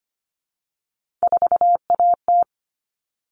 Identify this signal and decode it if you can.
Morse code, heard as a keyed tone, about 25 wpm, 700 Hz. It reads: 4AT
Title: All Cut Numbers - 3-Digit 25wpm